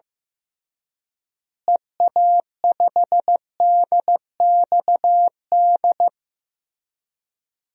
Morse code, heard as a keyed tone, about 16 wpm, 700 Hz. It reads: EA5DXD
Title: DXCC-100 Call Signs - 6 Characters - Repeated 3x - 15wpm